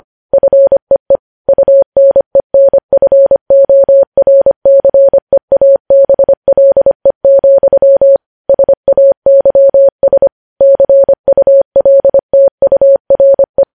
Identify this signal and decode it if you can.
Morse code, heard as a keyed tone, about 25 wpm, 550 Hz. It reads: FEE UNENFORCEABLE, SAYS CULTURE
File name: podcast / News headlines-1x-25x25